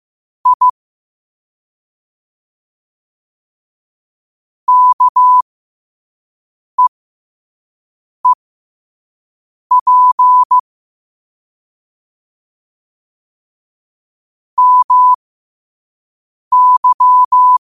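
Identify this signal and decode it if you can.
Morse code, heard as a keyed tone, about 15 wpm, spaced as for 5 wpm, 1000 Hz. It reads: I KEEP MY
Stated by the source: Morse code practice at 5 wpm.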